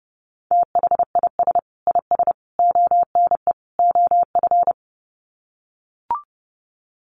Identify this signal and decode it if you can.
Morse code, heard as a keyed tone, about 28 wpm, 700 Hz. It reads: THIS IS ONE OF
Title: Sets of 4 Words 30wpm